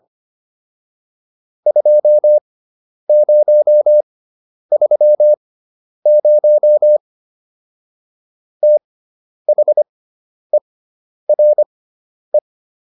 Morse code, heard as a keyed tone, about 25 wpm, spaced as for 10 wpm, 600 Hz. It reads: 2030 THERE